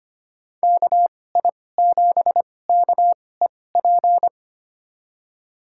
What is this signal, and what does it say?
Morse code, heard as a keyed tone, about 25 wpm, spaced as for 18 wpm, 700 Hz. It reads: KI7KEP
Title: USA Call Signs - 6 Characters - 2x Character Spacing - 25wpm